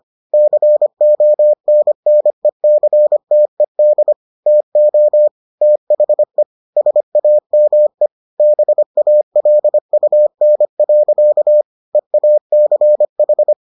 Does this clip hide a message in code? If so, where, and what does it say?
Yes: Morse code from 0.3 to 13.5 s, heard as a keyed tone, about 25 wpm, 600 Hz. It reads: CONNECTED TO THE SAME BALUN. EACH